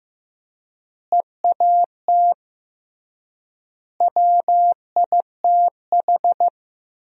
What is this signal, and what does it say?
Morse code, heard as a keyed tone, about 15 wpm, 700 Hz. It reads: EAT WITH
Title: 3x Word-Spacing - Sentences from Top 300 Words - 15wpm